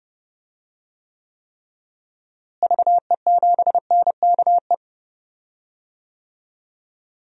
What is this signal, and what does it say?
Morse code, heard as a keyed tone, about 30 wpm, 700 Hz. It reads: VE7NKE
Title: DXCC-100 Call Signs - 6 Characters - Repeated 3x - 30wpm